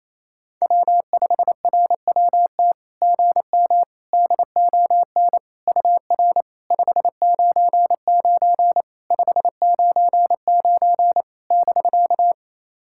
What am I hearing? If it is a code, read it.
Morse code, heard as a keyed tone, about 28 wpm, 660 Hz. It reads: W5RWT GM DON UR 599 599 <BK>
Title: SOTA QSO Practice 28wpm